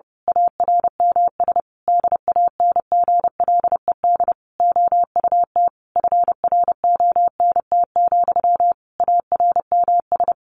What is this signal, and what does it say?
Morse code, heard as a keyed tone, about 30 wpm, 700 Hz. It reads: ARMS DANGLED OUT FRONT, ARMS
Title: A Sound Of Thunder 30wpm